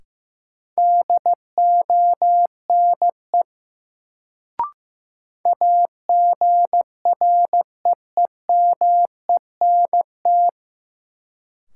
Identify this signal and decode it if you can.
Morse code, heard as a keyed tone, about 15 wpm, 700 Hz. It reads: DONE AGREEMENT
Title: Sets of 5 Words Encore - New Word Drill - 15wpm